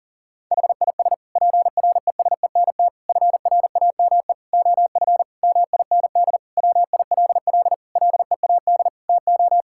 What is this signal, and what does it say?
Morse code, heard as a keyed tone, about 40 wpm, 700 Hz. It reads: HIS PRESENT FRAME OF MIND WILL LEAD TO
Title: Sentences from Top 1000 Words - Review Part 6 - 40wpm